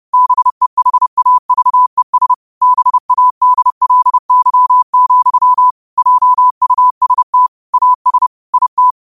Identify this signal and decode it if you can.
Morse code, heard as a keyed tone, about 30 wpm, 1000 Hz. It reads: BEHAVES BADLY, JUST AS IT
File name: qotdfiles / 30wpm